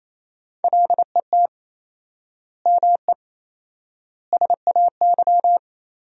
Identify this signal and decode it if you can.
Morse code, heard as a keyed tone, about 28 wpm, 700 Hz. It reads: LET ME SAY
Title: 4x Word-Spacing - Sentences from Top 300 Words - 28wpm